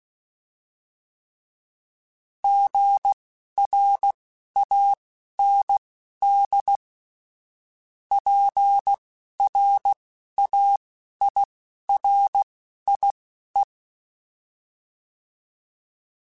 Morse code, heard as a keyed tone, about 16 wpm, spaced as for 11 wpm, 800 Hz. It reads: GRAND PRAIRIE